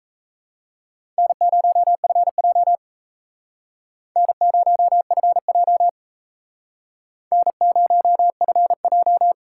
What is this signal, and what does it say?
Morse code, heard as a keyed tone, about 37 wpm, 700 Hz. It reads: N0FJ N0FJ N0FJ